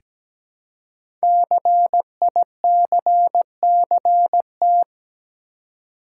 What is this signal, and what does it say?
Morse code, heard as a keyed tone, about 17 wpm, 700 Hz. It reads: CICCT
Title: Code Groups - 5 Letters - 17wpm